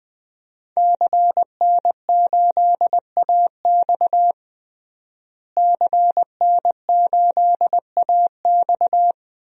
Morse code, heard as a keyed tone, about 20 wpm, 700 Hz. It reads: CN8AX CN8AX
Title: DXCC-100 Call Signs - 5 Characters - Repeated 2x - 20wpm